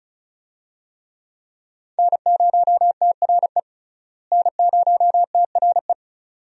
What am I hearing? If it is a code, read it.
Morse code, heard as a keyed tone, about 36 wpm, 700 Hz. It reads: N0TRE N0TRE
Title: USA Call Signs - 5 Characters - Repeated 2x - 35wpm